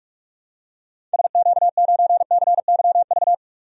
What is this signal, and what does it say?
Morse code, heard as a keyed tone, about 46 wpm, 700 Hz. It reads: SQ9CYV
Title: DXCC-100 Call Signs - 6 Characters - Repeated 3x - 45wpm